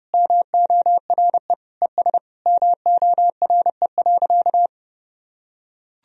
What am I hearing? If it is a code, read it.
Morse code, heard as a keyed tone, about 30 wpm, 700 Hz. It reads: MORE ES MORE.
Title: Sentences from Top 500 Words and Abbreviations 30wpm